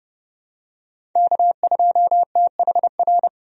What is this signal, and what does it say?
Morse code, heard as a keyed tone, about 30 wpm, 700 Hz. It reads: K2THR
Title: USA Call Signs - 5 Characters - 30wpm